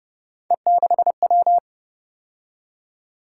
Morse code, heard as a keyed tone, about 31 wpm, 700 Hz. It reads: E6W